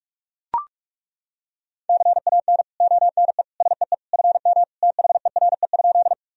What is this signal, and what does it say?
Morse code, heard as a keyed tone, about 41 wpm, 700 Hz. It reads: CAN ONE SEE FM THERE?
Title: Sentences from Top 100 Words and Abbreviations 45wpm